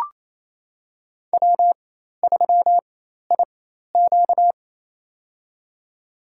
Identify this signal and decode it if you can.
Morse code, heard as a keyed tone, about 27 wpm, spaced as for 13 wpm, 700 Hz. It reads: W3IQ
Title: USA Call Signs - 4x Character Spacing - 28wpm